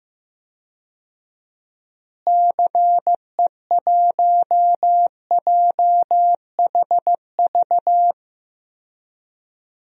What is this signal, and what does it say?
Morse code, heard as a keyed tone, about 15 wpm, 700 Hz. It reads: CE1JHV
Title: DXCC-100 Call Signs - 6 Characters - Repeated 3x - 15wpm